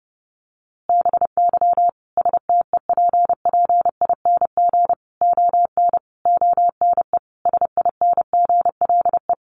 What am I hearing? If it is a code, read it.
Morse code, heard as a keyed tone, about 30 wpm, 700 Hz. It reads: BY STEPPING ON ONE SINGLE
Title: A Sound Of Thunder 30wpm